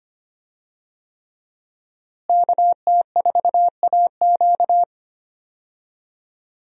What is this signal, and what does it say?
Morse code, heard as a keyed tone, about 25 wpm, 700 Hz. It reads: KT4AQ
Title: USA Call Signs - 5 Characters - Repeated 3x - 25wpm